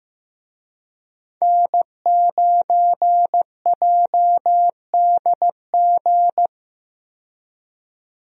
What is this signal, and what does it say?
Morse code, heard as a keyed tone, about 15 wpm, 700 Hz. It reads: N9JDG
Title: USA Call Signs - 5 Characters - 15wpm